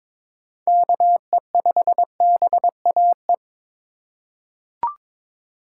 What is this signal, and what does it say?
Morse code, heard as a keyed tone, about 22 wpm, 700 Hz. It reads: KE5BAE
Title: USA Call Signs - 3x Character Spacing - 22wpm